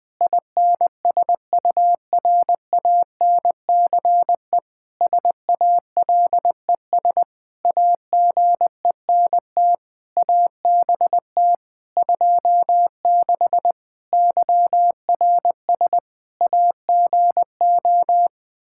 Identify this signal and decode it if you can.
Morse code, heard as a keyed tone, about 20 wpm, 700 Hz. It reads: INSURANCE SALES AGENT ABT 26 YRS AGO
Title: Practice QSO - Exchange 3 - More Info - 20wpm